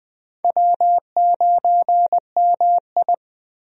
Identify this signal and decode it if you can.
Morse code, heard as a keyed tone, about 20 wpm, 700 Hz. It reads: W9MI
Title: USA Call Signs - 4 Characters - 20wpm